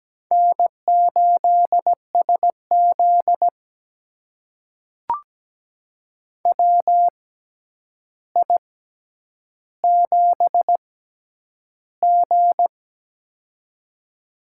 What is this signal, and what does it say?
Morse code, heard as a keyed tone, about 17 wpm, spaced as for 9 wpm, 700 Hz. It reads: N8SZ WI7G
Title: USA Call Signs - 4 Characters - 6x Character Spacing - 17wpm